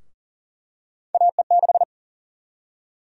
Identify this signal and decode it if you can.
Morse code, heard as a keyed tone, about 40 wpm, 700 Hz. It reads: AE6
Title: USA Call Sign Prefixes - Repeated 2x - 40wpm